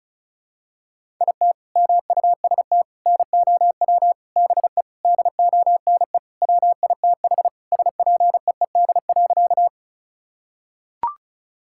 Morse code, heard as a keyed tone, about 36 wpm, 700 Hz. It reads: IT MUST NOW BE DONE WITH SPEED.
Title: Sentences from Top 1000 Words - Review Part 5 - 35wpm